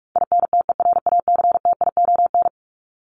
Morse code, heard as a keyed tone, about 44 wpm, 700 Hz. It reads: INTERACTION
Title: Top 1251-1500 Words 45wpm